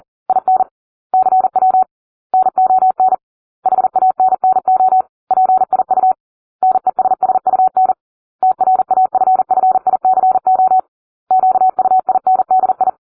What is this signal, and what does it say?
Morse code, heard as a keyed tone, about 40 wpm, 750 Hz. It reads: IN CW NON HANNO PIU NESSUN TRAFFICO QUINDI